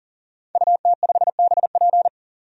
Morse code, heard as a keyed tone, about 40 wpm, 700 Hz. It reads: UT5BP
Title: DXCC-100 Call Signs - 5 Characters - 40wpm